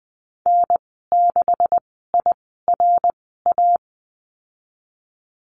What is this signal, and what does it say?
Morse code, heard as a keyed tone, about 20 wpm, spaced as for 15 wpm, 700 Hz. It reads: N6IRA
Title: 179 Lesson 29g - Slash - Call Signs 20wpm